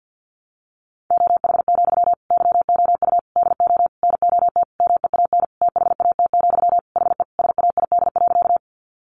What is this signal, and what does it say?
Morse code, heard as a keyed tone, about 50 wpm, 700 Hz. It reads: OH, YOU DO NOT MEAN THAT, HE SAID.
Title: Sentences from Top 600 Words 50wpm